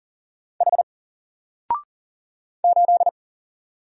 Morse code, heard as a keyed tone, about 40 wpm, 700 Hz. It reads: H 8
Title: Single Character (Letter:Number:Punctuation:Prosign) Alt 40wpm